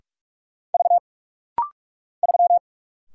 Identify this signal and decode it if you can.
Morse code, heard as a keyed tone, about 46 wpm, 700 Hz. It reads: V 3
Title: Single Letter-Number - Rapid-Fire 45wpm